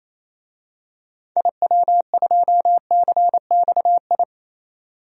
Morse code, heard as a keyed tone, about 27 wpm, 700 Hz. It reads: IW2CXI
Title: DXCC-100 Call Signs - 6 Characters - 28wpm